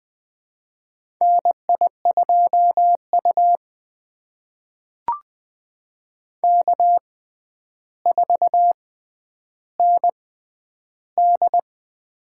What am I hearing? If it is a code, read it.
Morse code, heard as a keyed tone, about 20 wpm, spaced as for 10 wpm, 700 Hz. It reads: NI2U K4ND
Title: USA Call Signs - 4 Characters - 6x Character Spacing - 20wpm